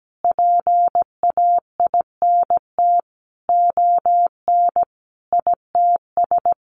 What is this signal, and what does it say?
Morse code, heard as a keyed tone, about 17 wpm, 700 Hz. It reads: PAINT ON ITS
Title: A Sound Of Thunder 17wpm